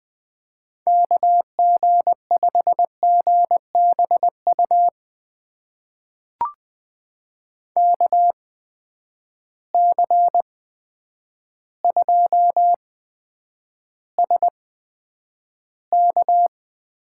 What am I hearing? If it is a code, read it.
Morse code, heard as a keyed tone, about 20 wpm, spaced as for 9 wpm, 700 Hz. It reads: KG5GBU KC2SK
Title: USA Call Signs - 6 Characters - 8x Character Spacing - 20wpm